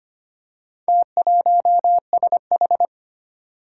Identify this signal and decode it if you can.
Morse code, heard as a keyed tone, about 25 wpm, 700 Hz. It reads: T1SH
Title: Four Challenging Letter-Number Combinations 25wpm